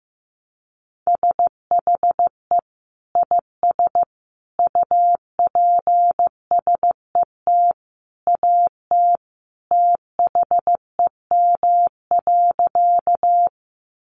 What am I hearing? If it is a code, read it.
Morse code, heard as a keyed tone, about 15 wpm, 700 Hz. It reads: SHE IS UPSET AT THEM.
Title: Sentences from Top 800 Words 15wpm